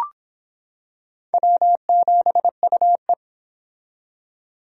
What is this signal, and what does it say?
Morse code, heard as a keyed tone, about 25 wpm, 700 Hz. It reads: W7UE